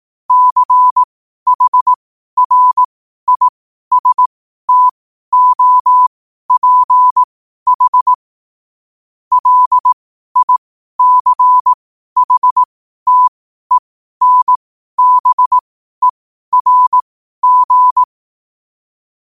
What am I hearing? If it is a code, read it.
Morse code, heard as a keyed tone, about 18 wpm, spaced as for 12 wpm, 1000 Hz. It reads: CHRISTOPH LICHTENBERG
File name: qotdfiles / 12wpm